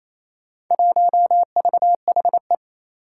Morse code, heard as a keyed tone, about 28 wpm, 700 Hz. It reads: 1VHE
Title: Four Challenging Letter-Number Combinations 28wpm